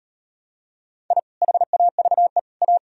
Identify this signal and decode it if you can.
Morse code, heard as a keyed tone, about 38 wpm, 700 Hz. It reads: I HAVE A